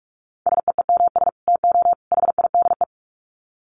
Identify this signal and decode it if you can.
Morse code, heard as a keyed tone, about 44 wpm, 700 Hz. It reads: SEEMS TO HIDE
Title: Dit Challenge 45wpm